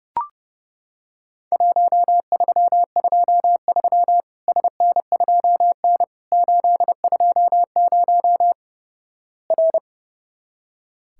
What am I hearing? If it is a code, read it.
Morse code, heard as a keyed tone, about 30 wpm, 700 Hz. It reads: 1323 SN2N 820 R
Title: Worked All Europe DX Contest - QTC Exchange - 30wpm